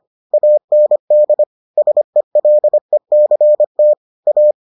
Morse code, heard as a keyed tone, about 25 wpm, 600 Hz. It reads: AND SELECT A